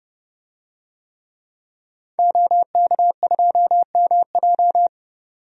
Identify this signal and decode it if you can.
Morse code, heard as a keyed tone, about 30 wpm, 700 Hz. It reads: OK2MJ
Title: DXCC-100 Call Signs - 5 Characters - 30wpm